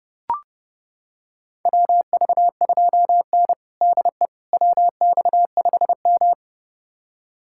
Morse code, heard as a keyed tone, about 32 wpm, 700 Hz. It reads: WV2N DE WX5M
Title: USA Callsign DE Callsign - 4 Characters - 30wpm